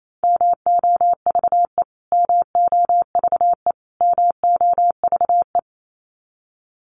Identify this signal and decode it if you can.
Morse code, heard as a keyed tone, about 28 wpm, 700 Hz. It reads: MOVE MOVE MOVE
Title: Top 500 Words Repeated 28wpm